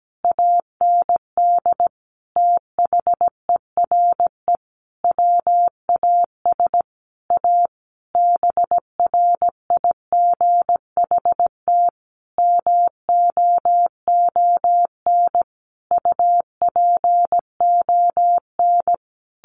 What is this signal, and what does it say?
Morse code, heard as a keyed tone, about 17 wpm, 700 Hz. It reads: AND THERE WAS A BRIGHT MOON UPON
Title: Sentences from Top 1500 Words 17wpm